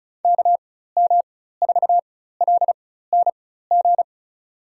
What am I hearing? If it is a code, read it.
Morse code, heard as a keyed tone, about 36 wpm, spaced as for 16 wpm, 700 Hz. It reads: KM4LNG
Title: USA Call Signs - 6 Characters - 4x Character Spacing - 35wpm